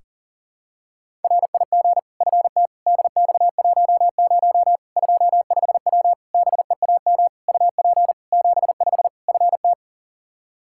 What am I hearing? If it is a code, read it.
Morse code, heard as a keyed tone, about 40 wpm, 700 Hz. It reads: RIG FT DX10 25W BEAM UP 75 FT
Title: Practice QSO - Exchange 2 - Info - 40wpm